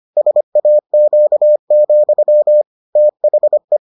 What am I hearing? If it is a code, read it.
Morse code, heard as a keyed tone, about 25 wpm, 600 Hz. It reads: SAQ, THE